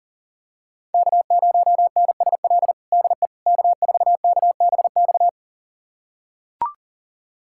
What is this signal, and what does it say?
Morse code, heard as a keyed tone, about 37 wpm, 700 Hz. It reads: K0NSL DE K4KBX
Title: USA Callsign DE Callsign - 5 Characters - 40wpm